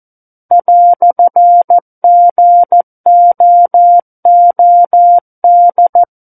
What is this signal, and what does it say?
Morse code, heard as a keyed tone, about 14 wpm, 700 Hz. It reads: "GOOD
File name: Book One / 05-Chapter 5 - 14 WPM